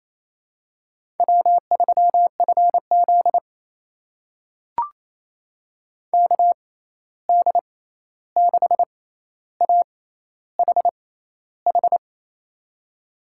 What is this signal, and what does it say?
Morse code, heard as a keyed tone, about 28 wpm, spaced as for 13 wpm, 700 Hz. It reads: W3FZ KD6AHH
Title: USA Call Signs - 6x Character Spacing - 28wpm